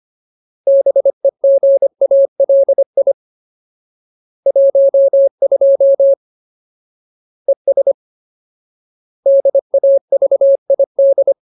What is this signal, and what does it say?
Morse code, heard as a keyed tone, about 25 wpm, 550 Hz. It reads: BEGALI 12 ES DAVID
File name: Ultimate Word List / 25WPM Ultimate